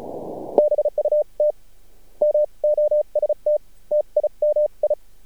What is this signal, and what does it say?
Morse code, heard as a keyed tone, about 34 wpm, spaced as for 30 wpm, 600 Hz. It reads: BUT MOST TIMI
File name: archived / CRSnet